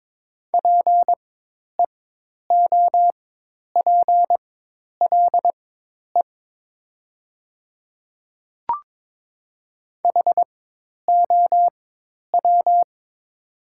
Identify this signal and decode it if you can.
Morse code, heard as a keyed tone, about 22 wpm, spaced as for 10 wpm, 700 Hz. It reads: PEOPLE HOW
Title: 4x Character Spacing - Top 100 Words - 22wpm